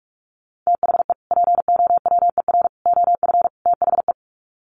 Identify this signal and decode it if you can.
Morse code, heard as a keyed tone, about 44 wpm, 700 Hz. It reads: THE POWER OF THE
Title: Sets of 4 Words 45wpm